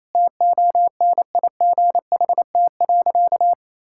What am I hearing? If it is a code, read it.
Morse code, heard as a keyed tone, about 28 wpm, 700 Hz. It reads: TONIGHT.
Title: Sentences from Top 1000 Words - Review Part 6 - 28wpm